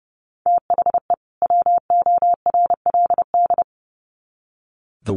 Morse code, heard as a keyed tone, about 30 wpm, 700 Hz. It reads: THE WORLD
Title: Sets of 2 Words 30wpm